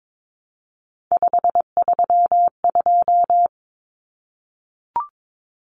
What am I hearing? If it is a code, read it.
Morse code, heard as a keyed tone, about 22 wpm, 700 Hz. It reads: 532
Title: Numbers - 3-Digit 22wpm